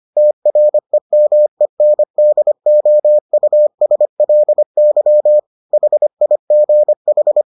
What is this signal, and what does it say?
Morse code, heard as a keyed tone, about 25 wpm, 600 Hz. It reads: TREMENDOUSLY HIGH